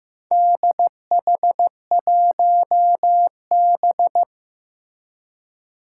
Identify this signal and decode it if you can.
Morse code, heard as a keyed tone, about 15 wpm, 700 Hz. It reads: DH1B
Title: Four Challenging Letter-Number Combinations 15wpm